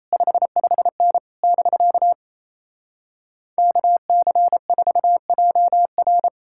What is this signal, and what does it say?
Morse code, heard as a keyed tone, about 37 wpm, 700 Hz. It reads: 55N <BK> KC4JR